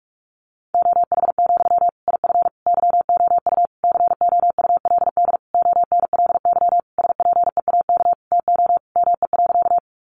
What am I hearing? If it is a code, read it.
Morse code, heard as a keyed tone, about 44 wpm, 700 Hz. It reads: OH, IF YOU COULD ONLY SPEAK TO ME.
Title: Sentences from Top 1100 Words 45wpm